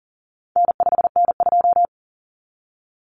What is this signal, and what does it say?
Morse code, heard as a keyed tone, about 40 wpm, 700 Hz. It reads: N5N2
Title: Common Cut Numbers - 4-Digit 40wpm